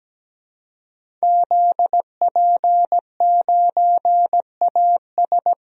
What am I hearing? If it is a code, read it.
Morse code, heard as a keyed tone, about 17 wpm, 700 Hz. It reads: ZP9AS
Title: DXCC-100 Call Signs - 5 Characters - Repeated 3x - 17wpm